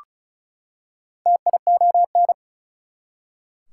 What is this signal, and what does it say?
Morse code, heard as a keyed tone, about 36 wpm, 700 Hz. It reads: TION
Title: Prefixes and Suffixes 35wpm